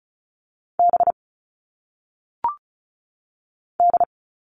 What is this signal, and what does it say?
Morse code, heard as a keyed tone, about 36 wpm, 700 Hz. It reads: B D
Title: All Cut Numbers - 1-Digit 35wpm